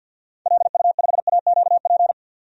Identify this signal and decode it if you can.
Morse code, heard as a keyed tone, about 54 wpm, 700 Hz. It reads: LU5AQP
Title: DXCC-100 Call Signs - 6 Characters - 50wpm